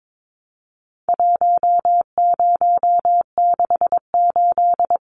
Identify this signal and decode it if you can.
Morse code, heard as a keyed tone, about 22 wpm, 700 Hz. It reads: 1068
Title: Numbers - 4-Digit 22wpm